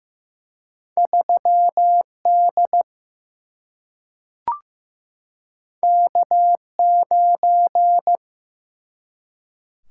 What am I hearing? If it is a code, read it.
Morse code, heard as a keyed tone, about 15 wpm, 700 Hz. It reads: 3D K9
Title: Single-Letter Single-Number 15wpm